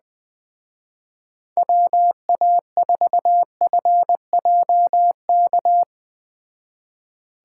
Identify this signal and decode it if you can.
Morse code, heard as a keyed tone, about 20 wpm, 700 Hz. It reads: WA4FJK